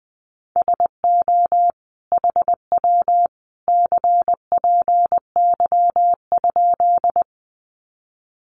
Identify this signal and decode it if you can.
Morse code, heard as a keyed tone, about 20 wpm, 700 Hz. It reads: SO HW CPY?
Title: Common QSO Phrases 20wpm